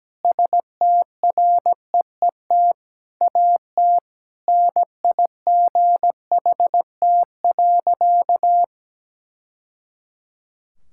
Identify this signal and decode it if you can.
Morse code, heard as a keyed tone, about 17 wpm, 700 Hz. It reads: STREET AT NIGHT.
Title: Sentences from Top 1000 Words - Review Part 6 - 17wpm